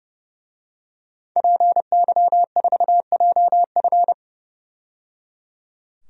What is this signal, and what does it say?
Morse code, heard as a keyed tone, about 30 wpm, 700 Hz. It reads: PY4JF
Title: DXCC-100 Call Signs - 5 Characters - 30wpm